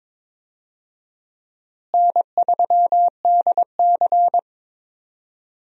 Morse code, heard as a keyed tone, about 21 wpm, 700 Hz. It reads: N3DC
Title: USA Call Signs - 4 Characters - Repeated 2x - 22wpm